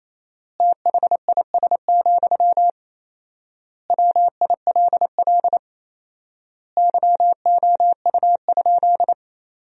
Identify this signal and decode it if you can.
Morse code, heard as a keyed tone, about 28 wpm, 700 Hz. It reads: THIS, WILL YOU?
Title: 4x Word-Spacing - Sentences from Top 400 Words - 28wpm